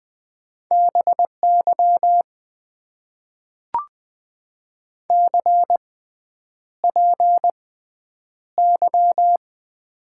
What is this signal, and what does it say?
Morse code, heard as a keyed tone, about 20 wpm, spaced as for 9 wpm, 700 Hz. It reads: BY CPY
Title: Top 90 QSO Elements - 6x Character Spacing - 20wpm